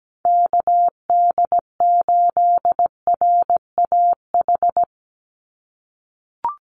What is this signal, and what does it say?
Morse code, heard as a keyed tone, about 17 wpm, 700 Hz. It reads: KD8RAH
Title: USA Call Signs 17wpm